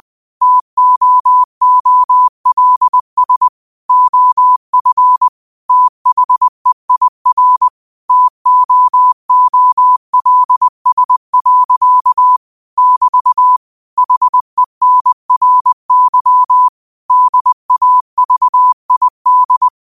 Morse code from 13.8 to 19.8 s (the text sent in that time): HENRY DAVID